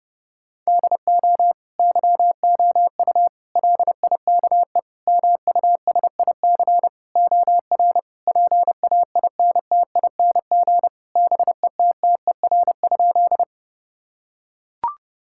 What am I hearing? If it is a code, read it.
Morse code, heard as a keyed tone, about 28 wpm, 700 Hz. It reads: DO YOU LIKE MUSIC OR PAINTING BETTER?
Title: Sentences from Top 1000 Words - Review Part 6 - 30wpm